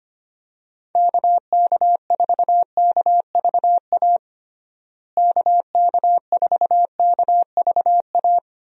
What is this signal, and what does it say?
Morse code, heard as a keyed tone, about 25 wpm, 700 Hz. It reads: KK4KVA KK4KVA
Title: USA Call Signs - 6 Characters - Repeated 2x - 25wpm